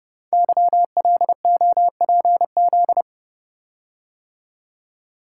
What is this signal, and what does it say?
Morse code, heard as a keyed tone, about 30 wpm, 700 Hz. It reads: YLOPZ
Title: Code Groups - 5 Letters - 30wpm